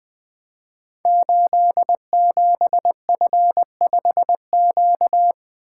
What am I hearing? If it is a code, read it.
Morse code, heard as a keyed tone, about 20 wpm, 700 Hz. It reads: 87F5Q
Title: Code Groups - 5 Letter:Numbers - 20wpm